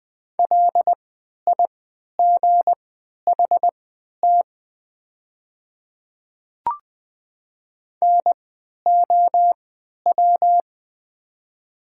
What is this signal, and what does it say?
Morse code, heard as a keyed tone, about 20 wpm, spaced as for 11 wpm, 690 Hz. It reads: LIGHT NOW
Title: 3x Character Spacing - Top 200 Words - 20wpm